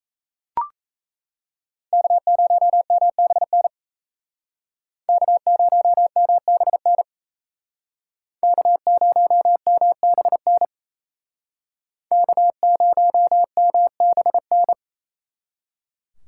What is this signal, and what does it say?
Morse code, heard as a keyed tone, about 37 wpm, 700 Hz. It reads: K0MBN K0MBN K0MBN K0MBN